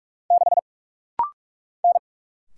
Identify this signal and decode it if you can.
Morse code, heard as a keyed tone, about 48 wpm, 700 Hz. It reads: 6 N
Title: Single Letter-Number - Rapid-Fire 45wpm